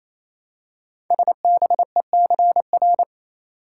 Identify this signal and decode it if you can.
Morse code, heard as a keyed tone, about 28 wpm, 700 Hz. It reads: SBECR
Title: Code Groups - 5 Letters - 28wpm